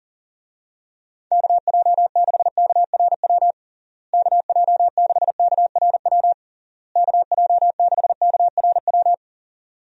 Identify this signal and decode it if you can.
Morse code, heard as a keyed tone, about 40 wpm, 700 Hz. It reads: KJ6KRW KJ6KRW KJ6KRW
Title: USA Call Signs - 6 Characters - Repeated 3x - 40wpm